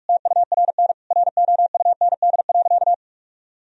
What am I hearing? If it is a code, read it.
Morse code, heard as a keyed tone, about 46 wpm, 700 Hz. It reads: TURN ROUND.
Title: Sentences from Top 1000 Words - Review Part 1 - 45wpm